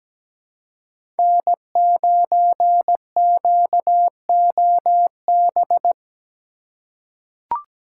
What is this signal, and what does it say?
Morse code, heard as a keyed tone, about 17 wpm, 700 Hz. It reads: N9QOB
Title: USA Call Signs - 5 Characters - 4x Character Spacing - 17wpm